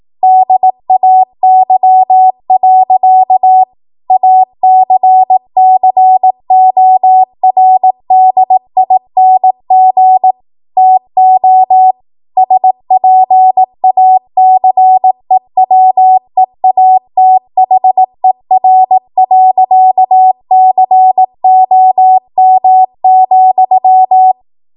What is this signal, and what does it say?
Morse code, heard as a keyed tone, about 18 wpm, 750 Hz. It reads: DAY. ACCORDING TO SPACEWEATHER.COM,
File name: Archive / 18 WPM / 18WPM